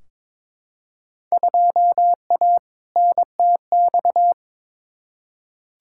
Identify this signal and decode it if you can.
Morse code, heard as a keyed tone, about 22 wpm, 700 Hz. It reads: 2A NTX
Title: Field Day Exchanges - 22wpm